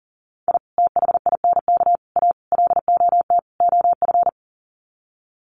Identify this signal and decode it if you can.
Morse code, heard as a keyed tone, about 40 wpm, 700 Hz. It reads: I THINK A LOT OF
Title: Sets of 5 Words 40wpm